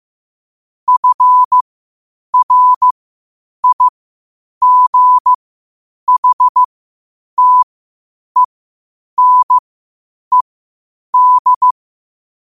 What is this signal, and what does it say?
Morse code, heard as a keyed tone, about 15 wpm, spaced as for 8 wpm, 1000 Hz. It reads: FRIGHTENED
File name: qotdfiles / 8wpm